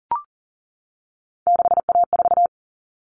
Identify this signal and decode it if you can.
Morse code, heard as a keyed tone, about 40 wpm, 700 Hz. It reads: 6A4
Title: Common Cut Numbers - 3-Digit 40wpm